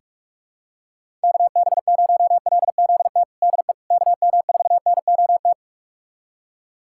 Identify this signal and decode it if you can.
Morse code, heard as a keyed tone, about 41 wpm, 700 Hz. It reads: KB0LZT DE KM4NOT
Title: USA Callsign DE Callsign - 6 Characters - 45wpm